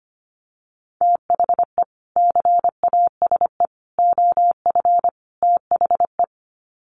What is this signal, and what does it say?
Morse code, heard as a keyed tone, about 25 wpm, 700 Hz. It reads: THE CASE OF THE
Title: Sets of 4 Words 25wpm